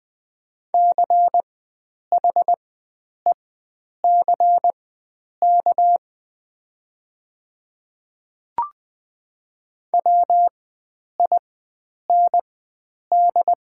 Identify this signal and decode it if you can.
Morse code, heard as a keyed tone, about 20 wpm, spaced as for 9 wpm, 700 Hz. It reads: CHECK WIND
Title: 4x Character Spacing - Top 500 Words - 20wpm